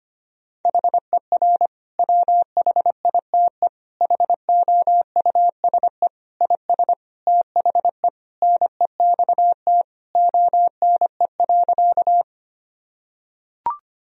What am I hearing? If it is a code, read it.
Morse code, heard as a keyed tone, about 25 wpm, 700 Hz. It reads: HER WHITE HOUSE IS THE NEXT ONE.
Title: Sentences from Top 1000 Words - Review Part 1 - 25wpm